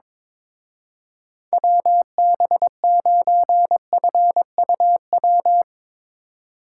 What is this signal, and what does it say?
Morse code, heard as a keyed tone, about 22 wpm, 700 Hz. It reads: WB9FUW
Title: Speed Racing - USA 2x3 Call Signs 22wpm